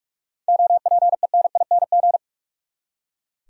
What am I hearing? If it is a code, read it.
Morse code, heard as a keyed tone, about 41 wpm, 700 Hz. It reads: OPENING
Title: Seven Letter Words - 45wpm